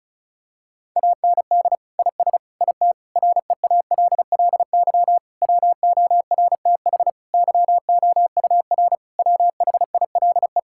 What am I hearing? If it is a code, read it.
Morse code, heard as a keyed tone, about 36 wpm, 700 Hz. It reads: AND IS IT REALLY WORTH YOUR WHILE
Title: Sentences from Top 1600 Words 35wpm